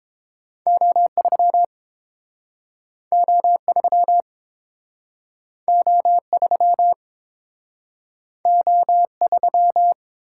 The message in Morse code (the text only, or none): O3 O3 O3 O3